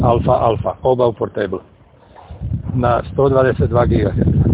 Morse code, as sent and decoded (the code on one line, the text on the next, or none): none